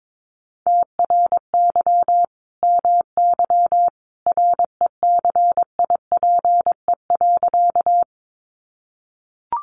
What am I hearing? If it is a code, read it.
Morse code, heard as a keyed tone, about 22 wpm, 700 Hz. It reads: TRY MY RECIPE.
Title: Sentences from Top 1200 Words 22wpm